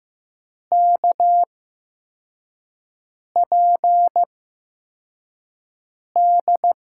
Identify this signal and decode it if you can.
Morse code, heard as a keyed tone, about 15 wpm, spaced as for 4 wpm, 700 Hz. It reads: KPD
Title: USA Call Signs - 6 Characters - 8x Character Spacing - 15wpm